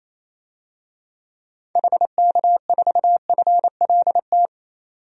Morse code, heard as a keyed tone, about 28 wpm, 700 Hz. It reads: HK4FLT
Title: DXCC-100 Call Signs - 6 Characters - 28wpm